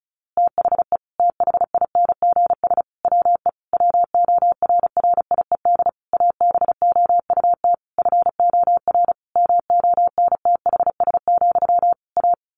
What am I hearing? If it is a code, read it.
Morse code, heard as a keyed tone, about 34 wpm, 700 Hz. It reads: THE THINGS WE WORRIED ABOUT FOR MONTHS, A
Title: A Sound Of Thunder 35wpm